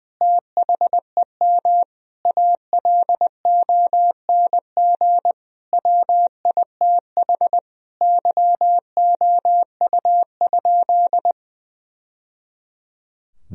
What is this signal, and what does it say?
Morse code, heard as a keyed tone, about 20 wpm, 700 Hz. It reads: THEM ALONG WITH YOU?
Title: Sentences from Top 1000 Words - Review Part 3 - 20wpm